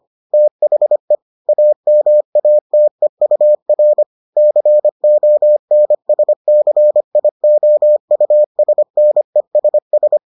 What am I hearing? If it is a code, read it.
Morse code, heard as a keyed tone, about 25 wpm, 600 Hz. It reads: THE AMATEUR CONSCIOUSNESS